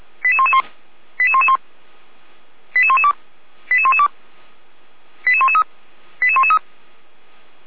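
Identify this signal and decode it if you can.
Five-tone selective call (ZVEI), heard as quick runs of five tones, tones of about 80 ms. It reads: 88111 88111 88112 88112 88113 88113
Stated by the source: ZVEI five-tone call received by SDR, from feuerwehr-orlamuende.org fire brigade audio archive.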